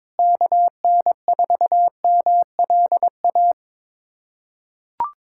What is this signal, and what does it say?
Morse code, heard as a keyed tone, about 22 wpm, 700 Hz. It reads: KN4MLA
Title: USA Callsign DE Callsign - 6 Characters - 22wpm